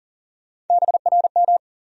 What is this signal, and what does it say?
Morse code, heard as a keyed tone, about 38 wpm, 700 Hz. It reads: BRM
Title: USA Call Sign Postfixes - 3 Letters 40wpm